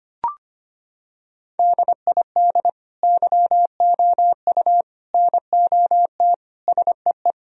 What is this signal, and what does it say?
Morse code, heard as a keyed tone, about 25 wpm, 700 Hz. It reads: DID YOU NOT SEE
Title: Sentences from Top 1000 Words - Review Part 2 - 25wpm